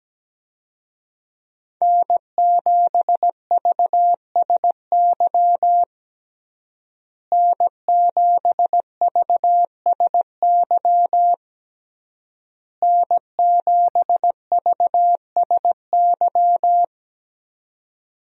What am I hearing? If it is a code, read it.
Morse code, heard as a keyed tone, about 17 wpm, 700 Hz. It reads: N7VSY N7VSY N7VSY
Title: USA Call Signs - 5 Characters - Repeated 3x - 17wpm